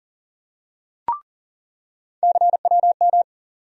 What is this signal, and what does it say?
Morse code, heard as a keyed tone, about 40 wpm, 700 Hz. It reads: CWM
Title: USA Call Sign Postfixes - 3 Letters 40wpm